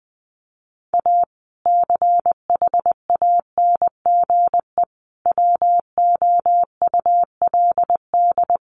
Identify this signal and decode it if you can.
Morse code, heard as a keyed tone, about 20 wpm, 700 Hz. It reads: A CHANGE WOULD
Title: Sentences from Top 1400 Words 20wpm